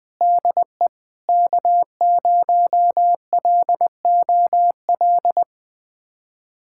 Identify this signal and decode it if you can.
Morse code, heard as a keyed tone, about 21 wpm, 700 Hz. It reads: DE K0LOL
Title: USA Callsign DE Callsign - 5 Characters - 20wpm